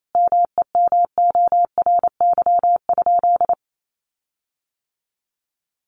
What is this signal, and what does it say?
Morse code, heard as a keyed tone, about 28 wpm, 700 Hz. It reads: MEMORY?
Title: Sentences from Top 1300 Words 28wpm